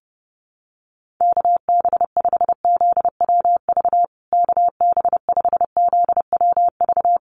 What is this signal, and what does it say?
Morse code, heard as a keyed tone, about 30 wpm, 700 Hz. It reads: KB5ZWV KB5ZWV
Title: USA Call Signs Repeated 3x 30wpm